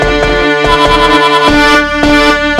There is no code.